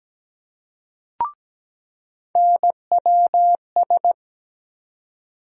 Morse code, heard as a keyed tone, about 17 wpm, 700 Hz. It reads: NWS
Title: USA Call Sign Postfixes - 3 Letters 17wpm